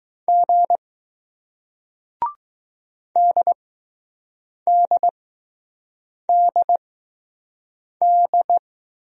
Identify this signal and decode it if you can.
Morse code, heard as a keyed tone, about 20 wpm, spaced as for 6 wpm, 700 Hz. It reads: G DDDD